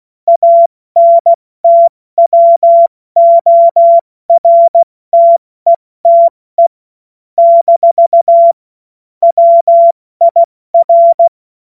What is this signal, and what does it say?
Morse code, heard as a keyed tone, about 16 wpm, 680 Hz. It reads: ANTWORTETE - WIR
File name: Morsekurs / Witze / Witz 25. Oktober 2022 70 F80